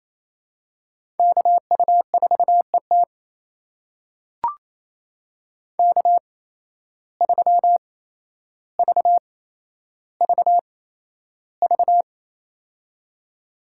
Morse code, heard as a keyed tone, about 27 wpm, spaced as for 12 wpm, 700 Hz. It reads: KU4ET K3VVV
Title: USA Call Signs - 5 Characters - 8x Character Spacing - 28wpm